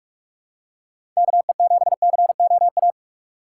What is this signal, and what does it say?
Morse code, heard as a keyed tone, about 46 wpm, 700 Hz. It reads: KE7COA
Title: USA Call Signs - 6 Characters - 6x Character Spacing - 45wpm